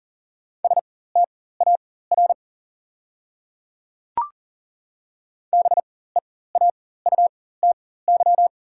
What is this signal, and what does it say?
Morse code, heard as a keyed tone, about 38 wpm, spaced as for 18 wpm, 700 Hz. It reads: STAR BEAUTY
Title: 4x Character Spacing - Top 500 Words - 40wpm